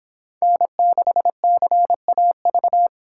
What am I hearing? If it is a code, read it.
Morse code, heard as a keyed tone, about 29 wpm, 700 Hz. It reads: N6CAV